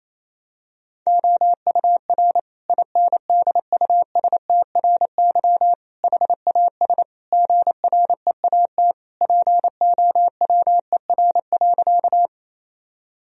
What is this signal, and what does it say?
Morse code, heard as a keyed tone, about 28 wpm, 700 Hz. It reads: OUR INDUSTRY HAS GREAT POWER.
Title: Sentences from Top 1000 Words - Review Part 4 - 28wpm